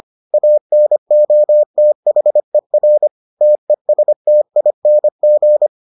Morse code, heard as a keyed tone, about 25 wpm, 600 Hz. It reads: ANOTHER TESTING